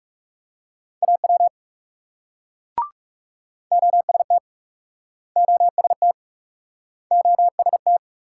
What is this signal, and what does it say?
Morse code, heard as a keyed tone, about 40 wpm, 700 Hz. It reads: AW OST OST OST